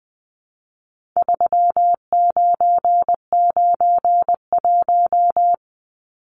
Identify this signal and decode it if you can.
Morse code, heard as a keyed tone, about 19 wpm, 700 Hz. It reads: 3991
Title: Numbers - 4-Digit 20wpm